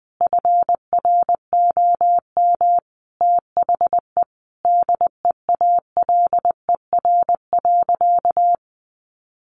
Morse code, heard as a keyed tone, about 20 wpm, 700 Hz. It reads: FROM THE DEALER.
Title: Sentences from Top 1100 Words 20wpm